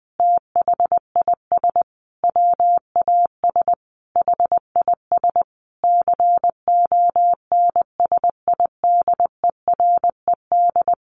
Morse code, heard as a keyed tone, about 20 wpm, 700 Hz. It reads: THIS WAS HIS CONSIDERED
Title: A Sound Of Thunder 20wpm